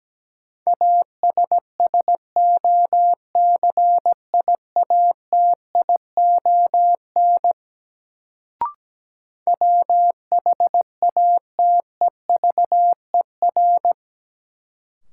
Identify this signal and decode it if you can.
Morse code, heard as a keyed tone, about 17 wpm, 700 Hz. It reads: ASSOCIATION WHATEVER
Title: Sentences from Top 1100 Words - New Word Drill Alt - 17wpm